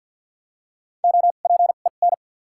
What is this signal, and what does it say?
Morse code, heard as a keyed tone, about 54 wpm, spaced as for 37 wpm, 700 Hz. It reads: OPEN
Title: 2x Character Spacing - Top 300 Words - 50wpm